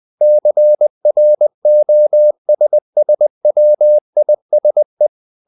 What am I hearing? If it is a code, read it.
Morse code, heard as a keyed tone, about 20 wpm, 600 Hz. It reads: CROSSWISE
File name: the old man and the sea / the old man and the sea part 1 20wpm